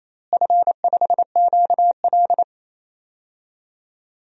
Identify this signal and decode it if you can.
Morse code, heard as a keyed tone, about 28 wpm, 700 Hz. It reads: F5QL
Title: DXCC-100 Call Signs - 4 Characters - Repeated 3x - 28wpm